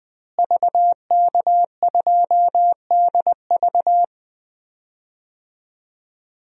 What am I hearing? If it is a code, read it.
Morse code, heard as a keyed tone, about 20 wpm, 700 Hz. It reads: VK2DV
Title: DXCC-100 Call Signs - 5 Characters - Repeated 2x - 20wpm